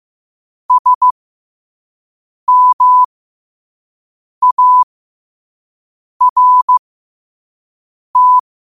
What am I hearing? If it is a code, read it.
Morse code, heard as a keyed tone, about 15 wpm, spaced as for 5 wpm, 1000 Hz. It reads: SMART
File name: qotdfiles / 5wpm